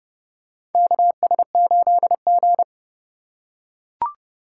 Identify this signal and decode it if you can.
Morse code, heard as a keyed tone, about 30 wpm, 700 Hz. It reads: KS8G
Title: USA Call Signs - 4 Characters - 2x Character Spacing - 30wpm